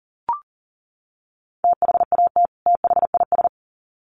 Morse code, heard as a keyed tone, about 40 wpm, 700 Hz. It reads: THAT THIS
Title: Sets of 2 Words 40wpm